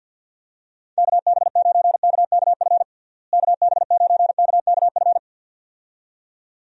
Morse code, heard as a keyed tone, about 50 wpm, 700 Hz. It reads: KB9KKF KB9KKF
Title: USA Call Signs - 6 Characters - Repeated 2x - 50wpm